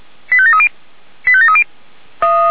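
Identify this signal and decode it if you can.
Five-tone selective call (ZVEI), heard as quick runs of five tones, tones of about 80 ms. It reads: 85730 85730
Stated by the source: ZVEI five-tone selective call received by SDR.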